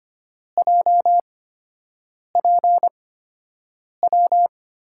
Morse code, heard as a keyed tone, about 25 wpm, spaced as for 7 wpm, 700 Hz. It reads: JPW
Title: USA Call Signs - 5 Characters - 8x Character Spacing - 25wpm